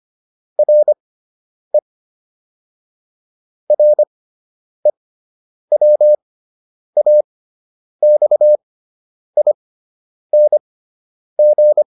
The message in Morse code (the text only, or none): RE REWAXING